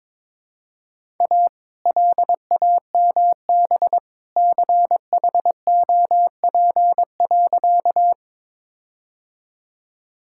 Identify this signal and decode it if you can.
Morse code, heard as a keyed tone, about 22 wpm, 700 Hz. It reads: A LAMB CHOP.
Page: morse-code-ninja-practice-22wpm